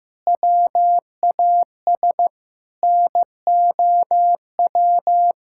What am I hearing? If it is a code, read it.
Morse code, heard as a keyed tone, about 15 wpm, 700 Hz. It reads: WAS NOW
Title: Sentences from Top 1000 Words - Review Part 6 - 15wpm